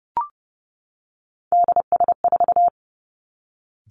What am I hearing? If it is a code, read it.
Morse code, heard as a keyed tone, about 30 wpm, 700 Hz. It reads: DS4
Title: Three Challenging Letter-Number Combinations 30wpm